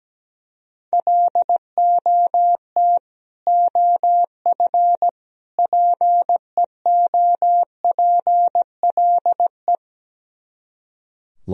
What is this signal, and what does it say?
Morse code, heard as a keyed tone, about 17 wpm, 700 Hz. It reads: LOT OF PEOPLE
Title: Speed-Racing - Sets of 3 Words 17wpm